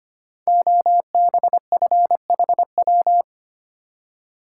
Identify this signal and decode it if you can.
Morse code, heard as a keyed tone, about 25 wpm, 700 Hz. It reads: OBFHW
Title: Code Groups - 5 Letters - 25wpm